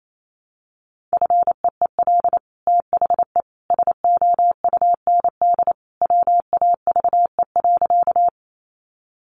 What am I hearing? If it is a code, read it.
Morse code, heard as a keyed tone, about 28 wpm, 700 Hz. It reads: FEEL THE SOUND WAVE.
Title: Sentences from Top 1400 Words 28wpm